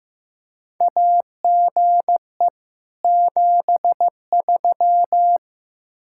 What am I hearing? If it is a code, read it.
Morse code, heard as a keyed tone, about 15 wpm, 700 Hz. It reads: AGE 73
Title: Practice QSO - Exchange 3 - More Info - 15wpm